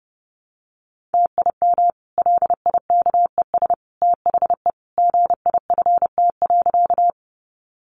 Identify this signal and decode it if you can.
Morse code, heard as a keyed tone, about 30 wpm, 700 Hz. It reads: TIM LIKES THE GIFT.
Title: Easy Phrases 30wpm